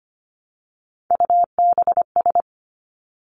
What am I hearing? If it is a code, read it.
Morse code, heard as a keyed tone, about 25 wpm, 700 Hz. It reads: UBS
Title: Three Challenging Letter-Number Combinations 25wpm